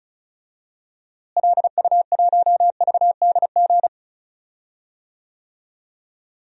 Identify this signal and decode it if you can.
Morse code, heard as a keyed tone, about 36 wpm, 700 Hz. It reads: LU1VDG
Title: DXCC-100 Call Signs - 6 Characters - Repeated 2x - 35wpm